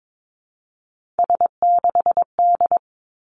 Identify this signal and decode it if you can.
Morse code, heard as a keyed tone, about 22 wpm, 700 Hz. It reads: S6D
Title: Three Challenging Letter-Number Combinations 22wpm